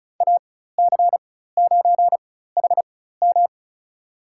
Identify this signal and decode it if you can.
Morse code, heard as a keyed tone, about 36 wpm, spaced as for 16 wpm, 700 Hz. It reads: AC9HM
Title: USA Call Signs - 5 Characters - 4x Character Spacing - 35wpm